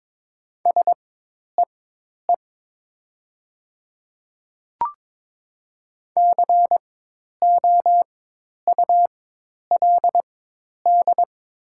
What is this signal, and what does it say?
Morse code, heard as a keyed tone, about 22 wpm, spaced as for 10 wpm, 700 Hz. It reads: SEE COULD
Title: 4x Character Spacing - Top 100 Words - 22wpm